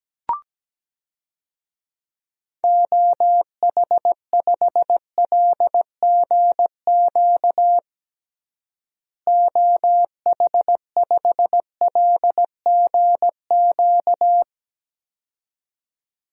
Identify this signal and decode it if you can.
Morse code, heard as a keyed tone, about 17 wpm, 700 Hz. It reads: OH5LGQ OH5LGQ
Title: DXCC-100 Call Signs - 6 Characters - Repeated 2x - 17wpm